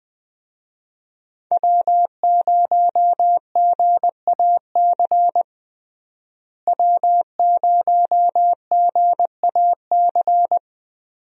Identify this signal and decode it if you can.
Morse code, heard as a keyed tone, about 20 wpm, 700 Hz. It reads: W0GAC W0GAC
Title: USA Call Signs - 5 Characters - Repeated 2x - 20wpm